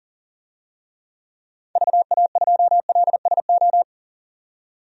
Morse code, heard as a keyed tone, about 40 wpm, 700 Hz. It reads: VA2LSO
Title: DXCC-100 Call Signs - 6 Characters - 40wpm